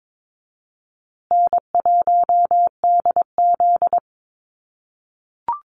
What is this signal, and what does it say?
Morse code, heard as a keyed tone, about 22 wpm, 700 Hz. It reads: N1DZ
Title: USA Call Signs Repeated 2x 22wpm